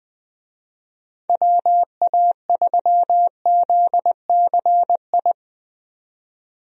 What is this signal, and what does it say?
Morse code, heard as a keyed tone, about 20 wpm, 700 Hz. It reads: WA3ZCI
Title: USA Call Signs - 6 Characters - 20wpm